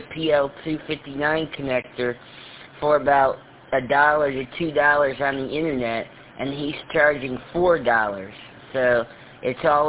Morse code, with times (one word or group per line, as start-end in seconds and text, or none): none